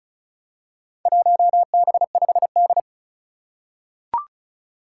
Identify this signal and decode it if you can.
Morse code, heard as a keyed tone, about 34 wpm, 700 Hz. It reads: 1B5D
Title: Four Challenging Letter-Number Combinations 35wpm